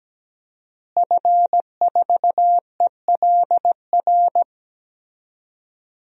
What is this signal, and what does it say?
Morse code, heard as a keyed tone, about 17 wpm, 700 Hz. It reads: F4ELR
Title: DXCC-100 Call Signs - 5 Characters - Repeated 3x - 17wpm